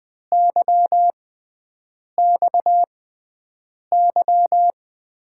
Morse code, heard as a keyed tone, about 20 wpm, spaced as for 7 wpm, 700 Hz. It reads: YXY
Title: USA Call Signs - 6x Character Spacing - 20wpm